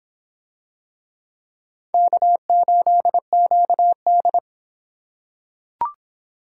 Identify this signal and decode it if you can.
Morse code, heard as a keyed tone, about 25 wpm, 700 Hz. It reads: K8QD